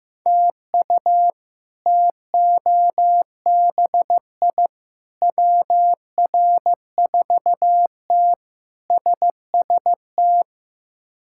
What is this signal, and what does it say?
Morse code, heard as a keyed tone, about 16 wpm, 700 Hz. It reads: TU TOBI WR4T SST
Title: K1USN SST QSO Practice - 15wpm